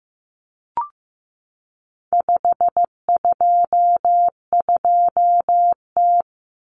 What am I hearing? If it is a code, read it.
Morse code, heard as a keyed tone, about 15 wpm, 700 Hz. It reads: 522T
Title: Common Cut Numbers - 4-Digit 15wpm